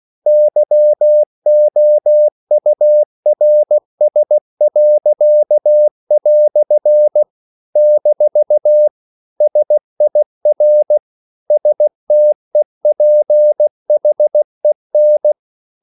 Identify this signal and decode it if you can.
Morse code, heard as a keyed tone, about 16 wpm, 600 Hz. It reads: YOURS." - SIR STEPHEN
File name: CW / Practice / EnglishQuotations0416